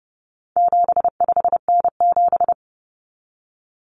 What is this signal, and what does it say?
Morse code, heard as a keyed tone, about 28 wpm, 700 Hz. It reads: 75N7
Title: Common Cut Numbers - 4-Digit 30wpm